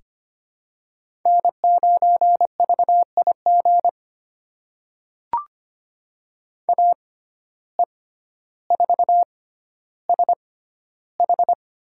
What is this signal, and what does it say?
Morse code, heard as a keyed tone, about 25 wpm, spaced as for 13 wpm, 700 Hz. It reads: N9VIG AE4SH
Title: USA Call Signs - 5 Characters - 6x Character Spacing - 25wpm